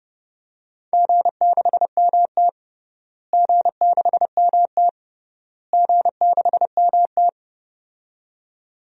Morse code, heard as a keyed tone, about 32 wpm, 700 Hz. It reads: G6MT G6MT G6MT
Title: DXCC-100 Call Signs - 4 Characters - Repeated 3x - 30wpm